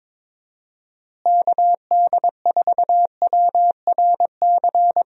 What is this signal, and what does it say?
Morse code, heard as a keyed tone, about 21 wpm, 700 Hz. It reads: KD4WRC
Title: USA Call Signs - 6 Characters - Repeated 3x - 22wpm